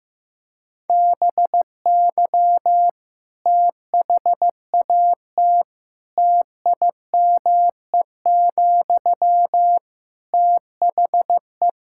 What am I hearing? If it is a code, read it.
Morse code, heard as a keyed tone, about 15 wpm, 700 Hz. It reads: BY THAT TIME, THE
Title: Sentences from Top 1600 Words 15wpm